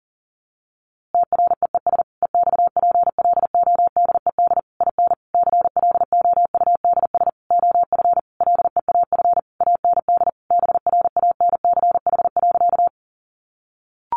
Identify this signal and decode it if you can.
Morse code, heard as a keyed tone, about 40 wpm, 700 Hz. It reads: TREES EXPLODED IN CLOUDS OF LEAF AND BRANCH.
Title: A Sound Of Thunder 40wpm